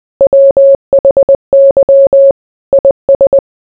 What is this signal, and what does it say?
Morse code, heard as a keyed tone, about 20 wpm, 550 Hz. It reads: WHY IS